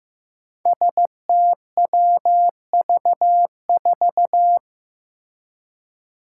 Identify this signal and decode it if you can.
Morse code, heard as a keyed tone, about 15 wpm, 700 Hz. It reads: STWV4
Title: Five Challenging Letter-Number Combinations 15wpm